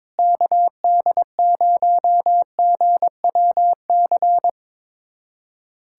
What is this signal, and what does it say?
Morse code, heard as a keyed tone, about 22 wpm, 700 Hz. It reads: KD0GWC
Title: USA Callsign DE Callsign - 6 Characters - 22wpm